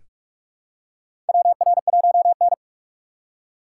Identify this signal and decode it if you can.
Morse code, heard as a keyed tone, about 41 wpm, 700 Hz. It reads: WR1N
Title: USA Call Signs - 4 Characters - 3x Character Spacing - 45wpm